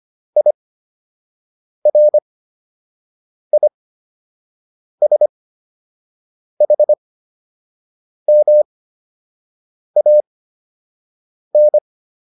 Morse code, heard as a keyed tone, about 25 wpm, spaced as for 6 wpm, 600 Hz. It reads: IRISHMAN